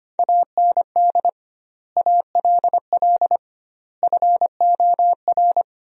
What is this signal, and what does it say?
Morse code, heard as a keyed tone, about 25 wpm, 700 Hz. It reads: AND ALL FOR
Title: 2x Word-Spacing - Sentences from Top 100 Words - 25wpm